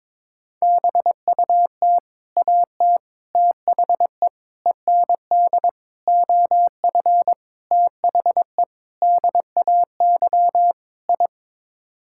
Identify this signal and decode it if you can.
Morse code, heard as a keyed tone, about 21 wpm, 700 Hz. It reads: BUT AT THE END OF THE DAY I
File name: practice / 22wpm / Sets of 8 Words - 22wpm